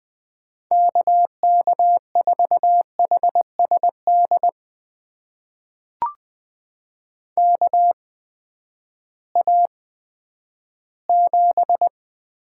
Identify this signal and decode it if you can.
Morse code, heard as a keyed tone, about 20 wpm, spaced as for 11 wpm, 700 Hz. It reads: KK4HSD KA7
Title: USA Call Signs - 6 Characters - 8x Character Spacing - 20wpm